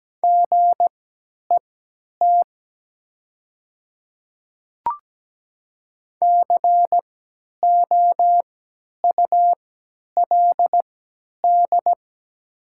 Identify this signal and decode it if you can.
Morse code, heard as a keyed tone, about 17 wpm, spaced as for 10 wpm, 700 Hz. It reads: GET COULD
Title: 3x Character Spacing - Top 500 Words - 17wpm